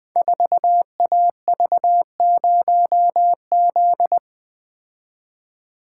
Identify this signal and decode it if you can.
Morse code, heard as a keyed tone, about 20 wpm, 700 Hz. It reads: 4AV0Z
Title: Code Groups - 5 Letter:Numbers - 20wpm